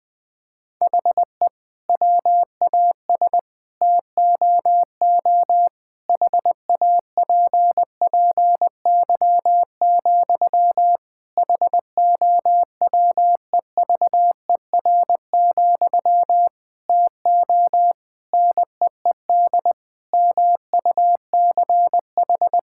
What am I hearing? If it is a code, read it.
Morse code, heard as a keyed tone, about 20 wpm, 700 Hz. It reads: HE WAS TOO HAPPY, HOWEVER, TO NEED MUCH
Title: Sentences from Top 1000 Words - Review Part 6 - 20wpm